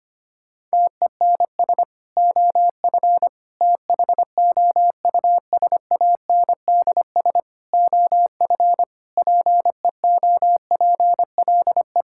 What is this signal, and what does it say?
Morse code, heard as a keyed tone, about 25 wpm, 700 Hz. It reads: TENS OF THOUSANDS OF PEOPLE
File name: practice / 25wpm / Sentences with 4-Word Phrases - 25wpm